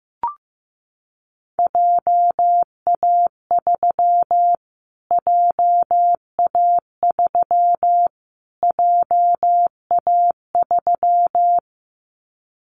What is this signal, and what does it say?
Morse code, heard as a keyed tone, about 15 wpm, 700 Hz. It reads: JA3 JA3 JA3
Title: Callsign Prefixes for Top 25 DXCC - Repeated 3x - 15wpm